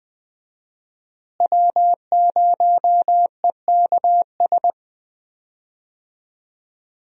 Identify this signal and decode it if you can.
Morse code, heard as a keyed tone, about 21 wpm, 700 Hz. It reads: W0EKS
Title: USA Call Signs - 5 Characters - Repeated 3x - 20wpm